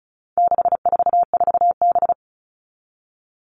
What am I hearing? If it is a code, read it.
Morse code, heard as a keyed tone, about 34 wpm, 700 Hz. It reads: 644B
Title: All Cut Numbers - 4-Digit 35wpm